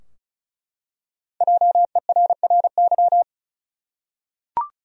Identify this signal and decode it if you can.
Morse code, heard as a keyed tone, about 36 wpm, 700 Hz. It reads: JERRY
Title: USA Common Names - Max 5 Letters - 35wpm